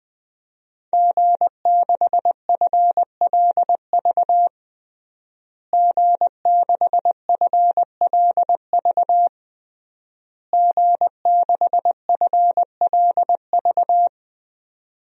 Morse code, heard as a keyed tone, about 20 wpm, 700 Hz. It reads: G6FLV G6FLV G6FLV
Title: DXCC-100 Call Signs - 5 Characters - Repeated 3x - 20wpm